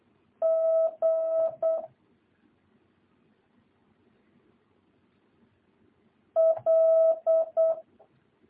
Morse code, heard as a keyed tone, about 8 wpm, 650 Hz. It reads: G L